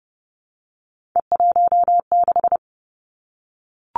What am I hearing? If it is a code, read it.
Morse code, heard as a keyed tone, about 30 wpm, 700 Hz. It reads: E16
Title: Three Challenging Letter-Number Combinations 30wpm